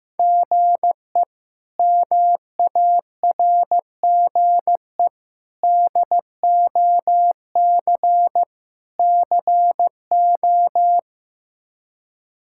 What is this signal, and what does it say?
Morse code, heard as a keyed tone, about 15 wpm, 700 Hz. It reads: GE MARGE DOC CO
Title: K1USN SST QSO Practice - 15wpm